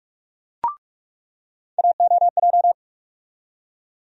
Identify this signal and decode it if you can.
Morse code, heard as a keyed tone, about 41 wpm, 700 Hz. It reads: AOJ
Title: USA Call Sign Postfixes - 3 Letters 45wpm